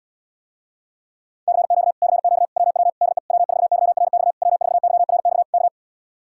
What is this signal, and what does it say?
Morse code, heard as a keyed tone, about 77 wpm, 700 Hz. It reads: CQ CQ WW DE W4YRX W4YRX K